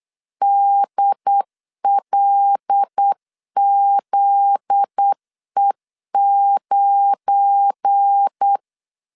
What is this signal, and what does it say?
Morse code, heard as a keyed tone, about 8 wpm, 800 Hz. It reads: DLZE9
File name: LEZIONI / 7 GF9Z / 7 GF9Z 8wpm 3213